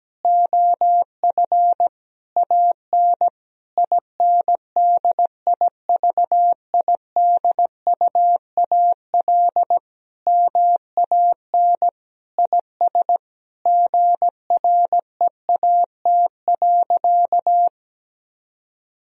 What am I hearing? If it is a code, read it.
Morse code, heard as a keyed tone, about 17 wpm, 700 Hz. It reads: OF AN INDIVIDUAL MAN IS GREAT.
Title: Sentences from Top 1000 Words - Review Part 4 - 17wpm